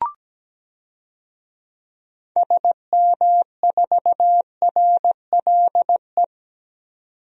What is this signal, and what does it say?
Morse code, heard as a keyed tone, about 17 wpm, 700 Hz. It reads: SM4RLE
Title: DXCC-100 Call Signs - 6 Characters - 17wpm